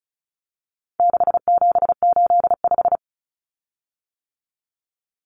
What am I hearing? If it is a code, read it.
Morse code, heard as a keyed tone, about 34 wpm, 700 Hz. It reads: 6785
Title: Common Cut Numbers - 4-Digit 35wpm